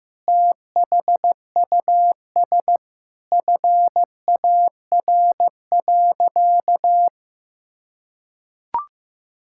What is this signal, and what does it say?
Morse code, heard as a keyed tone, about 15 wpm, 700 Hz. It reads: THUS FAR.
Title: Sentences from Top 1000 Words - Review Part 5 - 15wpm